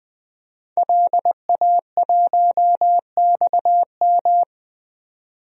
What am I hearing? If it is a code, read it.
Morse code, heard as a keyed tone, about 20 wpm, 700 Hz. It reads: LA1XM
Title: DXCC-100 Call Signs - 5 Characters - 20wpm